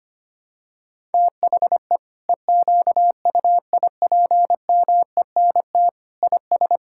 Morse code, heard as a keyed tone, about 25 wpm, 700 Hz. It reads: THE EQUIPMENT IS
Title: Sentences from Top 1000 Words - Review Part 5 - 25wpm